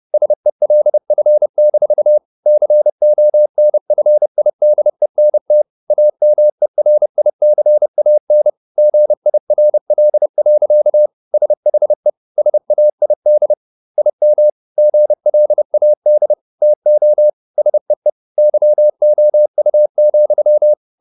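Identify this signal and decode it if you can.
Morse code, heard as a keyed tone, about 30 wpm, 600 Hz. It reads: SELF-CONFIDENT AMERICAN GIRL. SHE SAID IM GLAD TO SEE YOU,
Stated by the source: Morse code at 30 wpm, the novel Never by Ken Follett in English, from CW med Gnister.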